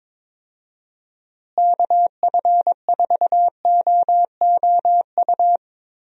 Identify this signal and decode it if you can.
Morse code, heard as a keyed tone, about 22 wpm, 700 Hz. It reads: KF4OOU
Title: USA Call Signs - 6 Characters - Repeated 2x - 22wpm